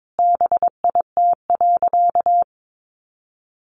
Morse code, heard as a keyed tone, about 22 wpm, 700 Hz. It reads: BIT.
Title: Sentences from Top 900 Words 22wpm